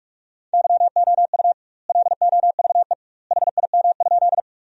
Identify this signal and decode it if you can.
Morse code, heard as a keyed tone, about 44 wpm, 700 Hz. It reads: YOU LOVE HIM?
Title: 2x Word-Spacing - Sentences from Top 400 Words - 45wpm